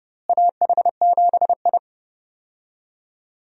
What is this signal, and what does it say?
Morse code, heard as a keyed tone, about 30 wpm, 700 Hz. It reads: AH7I
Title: USA Call Signs - 4 Characters - Repeated 3x - 30wpm